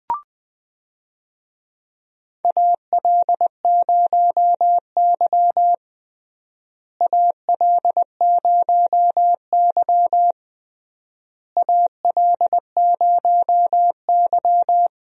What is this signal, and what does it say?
Morse code, heard as a keyed tone, about 20 wpm, 700 Hz. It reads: AL0Y AL0Y AL0Y
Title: USA Call Signs - 4 Characters - Repeated 3x - 20wpm